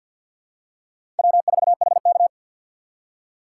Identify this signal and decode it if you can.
Morse code, heard as a keyed tone, about 52 wpm, 700 Hz. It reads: W4HK
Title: USA Call Signs - 4 Characters - 3x Character Spacing - 50wpm